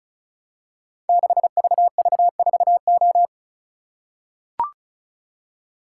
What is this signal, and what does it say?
Morse code, heard as a keyed tone, about 36 wpm, 700 Hz. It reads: 6VV4O
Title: Code Groups - 5 Letter:Numbers - 35wpm